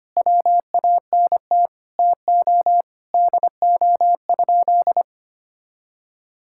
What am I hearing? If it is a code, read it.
Morse code, heard as a keyed tone, about 25 wpm, 700 Hz. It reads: WANT TO DO?
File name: practice / 25wpm / Sentences with 3-Word Phrases - 25wpm